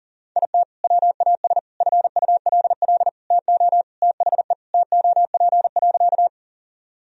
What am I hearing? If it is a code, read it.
Morse code, heard as a keyed tone, about 40 wpm, 700 Hz. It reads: IT WAS FULL TO THE TOP.
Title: Sentences from Top 1000 Words - Review Part 3 - 40wpm